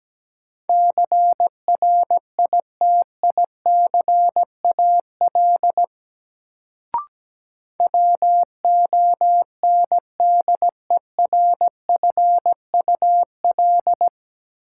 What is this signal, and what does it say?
Morse code, heard as a keyed tone, about 17 wpm, 700 Hz. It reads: CRITICAL WONDERFUL
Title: Sentences from Top 1100 Words - New Word Drill - 17wpm